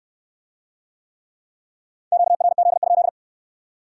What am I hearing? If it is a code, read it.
Morse code, heard as a keyed tone, about 64 wpm, 700 Hz. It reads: QRZ?